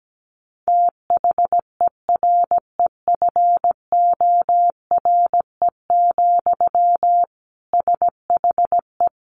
Morse code, heard as a keyed tone, about 17 wpm, 700 Hz. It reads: THEREFORE, SHE
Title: Sentences from Top 800 Words 17wpm